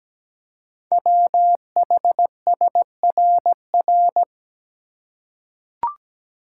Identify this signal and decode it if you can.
Morse code, heard as a keyed tone, about 17 wpm, 700 Hz. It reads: WHSRR
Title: Code Groups - 5 Letters - 17wpm